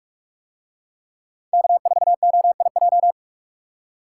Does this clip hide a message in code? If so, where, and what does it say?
Yes: Morse code from 1.5 to 3.1 s, heard as a keyed tone, about 42 wpm, 700 Hz. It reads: K4OIJ